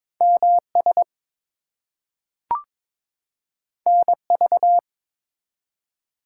Morse code, heard as a keyed tone, about 22 wpm, 700 Hz. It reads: MS NV
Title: US States and Canadian Provinces 22wpm